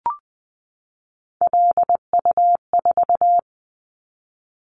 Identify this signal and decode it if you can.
Morse code, heard as a keyed tone, about 20 wpm, 700 Hz. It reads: LU4
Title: Callsign Prefixes for Top 50 DXCC - 20wpm